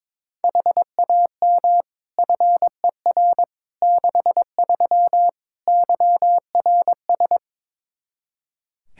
Morse code, heard as a keyed tone, about 22 wpm, 700 Hz. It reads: HAM FER 63 YRS
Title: Practice QSO - Exchange 3 - More Info - 22wpm